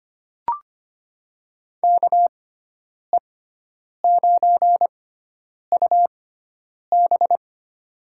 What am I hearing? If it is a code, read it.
Morse code, heard as a keyed tone, about 25 wpm, spaced as for 9 wpm, 700 Hz. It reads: KE9UB
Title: USA Call Signs - 5 Characters - 6x Character Spacing - 25wpm